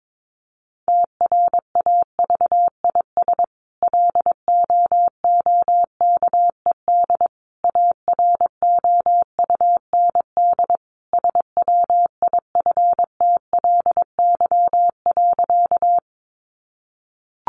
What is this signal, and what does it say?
Morse code, heard as a keyed tone, about 22 wpm, 700 Hz. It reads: TRAVIS LOOKED AROUND SWIFTLY.
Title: A Sound Of Thunder 22wpm